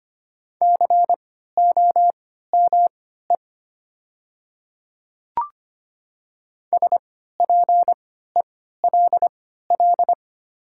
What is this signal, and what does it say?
Morse code, heard as a keyed tone, about 25 wpm, spaced as for 14 wpm, 700 Hz. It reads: COME SPELL
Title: 3x Character Spacing - Top 300 Words - 25wpm